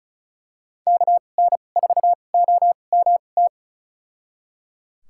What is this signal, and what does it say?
Morse code, heard as a keyed tone, about 34 wpm, spaced as for 25 wpm, 690 Hz. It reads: KN4OMT
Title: USA Call Signs - 6 Characters - 2x Character Spacing - 35wpm